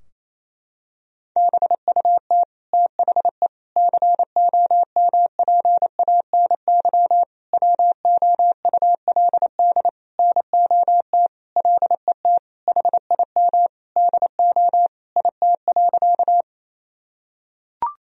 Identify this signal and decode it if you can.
Morse code, heard as a keyed tone, about 27 wpm, 700 Hz. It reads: BUT THE COMPANY WOULD NOT LET HIM DO IT.
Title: Sentences from Top 1000 Words - Review Part 3 - 28wpm